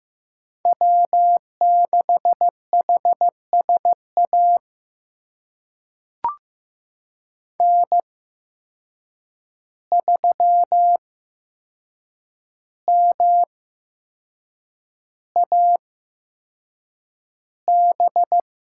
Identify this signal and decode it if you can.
Morse code, heard as a keyed tone, about 15 wpm, spaced as for 6 wpm, 700 Hz. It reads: W6HSA N3MAB
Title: USA Call Signs - 5 Characters - 8x Character Spacing - 15wpm